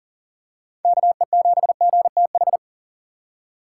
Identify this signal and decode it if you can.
Morse code, heard as a keyed tone, about 40 wpm, 700 Hz. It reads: KE7GTH
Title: USA Call Signs - 6 Characters - 6x Character Spacing - 40wpm